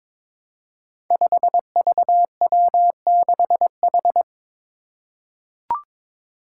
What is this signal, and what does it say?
Morse code, heard as a keyed tone, about 22 wpm, 700 Hz. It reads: 5VW6H
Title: Five Challenging Letter-Number Combinations 22wpm